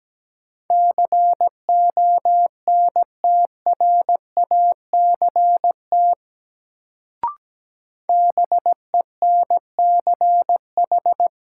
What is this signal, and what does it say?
Morse code, heard as a keyed tone, about 17 wpm, 700 Hz. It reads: CONTRACT BENCH
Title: Sentences from Top 1400 Words - New Word Drill - 17wpm